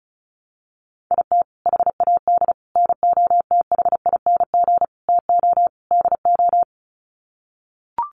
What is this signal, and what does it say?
Morse code, heard as a keyed tone, about 34 wpm, 700 Hz. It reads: IT HAD NOTHING TO DO
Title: Sets of 5 Words Encore2 35wpm